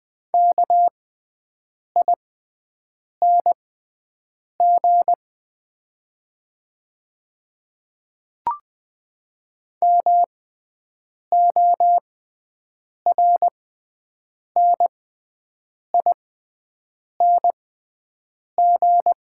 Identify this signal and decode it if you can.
Morse code, heard as a keyed tone, about 20 wpm, spaced as for 7 wpm, 690 Hz. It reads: KING MORNING
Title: 6x Character Spacing - Top 500 Words - 20wpm